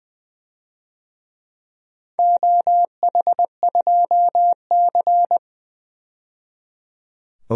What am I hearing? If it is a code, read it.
Morse code, heard as a keyed tone, about 20 wpm, 700 Hz. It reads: OH2C
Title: DXCC-100 Call Signs - 4 Characters - 20wpm